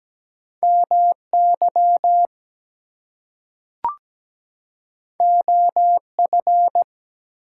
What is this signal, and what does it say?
Morse code, heard as a keyed tone, about 17 wpm, 700 Hz. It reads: MY OF
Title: Two Letter Words - 17wpm